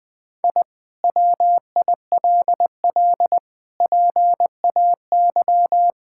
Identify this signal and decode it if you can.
Morse code, heard as a keyed tone, about 20 wpm, 700 Hz. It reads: I WILL PAY
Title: Sentences from Top 1000 Words - Review Part 5 - 20wpm